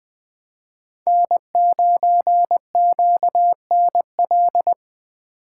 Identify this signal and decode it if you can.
Morse code, heard as a keyed tone, about 20 wpm, 700 Hz. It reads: N9QNL
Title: Speed Racing - USA Call Signs 17wpm